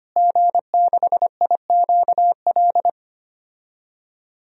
Morse code, heard as a keyed tone, about 25 wpm, 700 Hz. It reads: G6IQL
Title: DXCC-100 Call Signs - 5 Characters - Repeated 2x - 25wpm